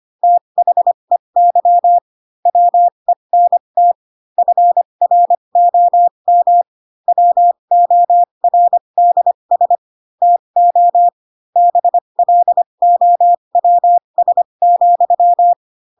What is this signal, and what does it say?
Morse code, heard as a keyed tone, about 25 wpm, spaced as for 22 wpm, 700 Hz. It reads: THEY WENT FROM WORDS TO BLOWS,
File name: Pinocchio / Pinocchio ch2 18 25 ik1hge